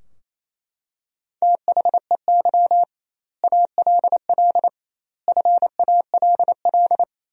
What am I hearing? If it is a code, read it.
Morse code, heard as a keyed tone, about 28 wpm, 700 Hz. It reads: THEY ALL FALL
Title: 2x Word-Spacing - Sentences from Top 500 Words - 28wpm